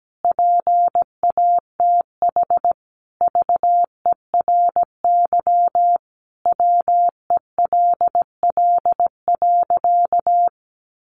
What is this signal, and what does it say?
Morse code, heard as a keyed tone, about 17 wpm, 700 Hz. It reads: PATH VERY WELL.
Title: Sentences from Top 1400 Words 17wpm